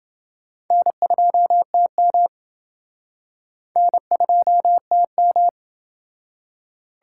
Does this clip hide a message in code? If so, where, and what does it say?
Yes: Morse code from 0.7 to 5.5 s, heard as a keyed tone, about 28 wpm, 700 Hz. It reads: N2TM N2TM